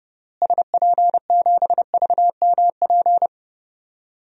Morse code, heard as a keyed tone, about 30 wpm, 700 Hz. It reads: SP7VMP
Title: DXCC-100 Call Signs - 6 Characters - Repeated 3x - 30wpm